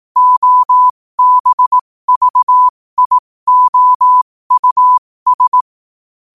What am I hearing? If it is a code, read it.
Morse code, heard as a keyed tone, about 18 wpm, spaced as for 15 wpm, 1000 Hz. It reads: OBVIOUS
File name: qotdfiles / 15wpm